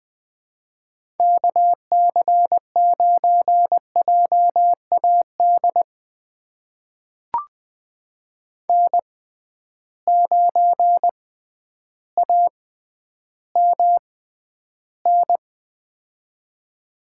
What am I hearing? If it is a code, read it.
Morse code, heard as a keyed tone, about 20 wpm, spaced as for 11 wpm, 700 Hz. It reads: KC9JAD N9AMN
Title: USA Call Signs - 6x Character Spacing - 20wpm